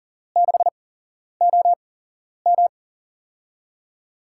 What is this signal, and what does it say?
Morse code, heard as a keyed tone, about 40 wpm, spaced as for 11 wpm, 700 Hz. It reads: 6OM
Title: USA Call Signs - 5 Characters - 8x Character Spacing - 40wpm